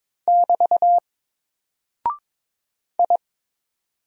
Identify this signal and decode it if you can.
Morse code, heard as a keyed tone, about 22 wpm, 700 Hz. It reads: <BT> I
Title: Single Character (Letter:Number:Punctuation:Prosign) 22wpm